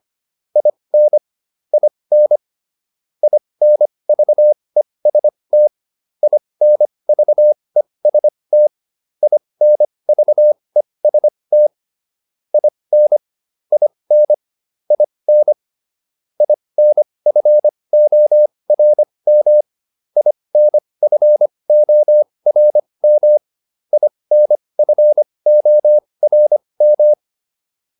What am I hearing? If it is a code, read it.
Morse code, heard as a keyed tone, about 25 wpm, spaced as for 20 wpm, 600 Hz. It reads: IN IN INVEST INVEST INVEST IN IN IN INFORM INFORM INFORM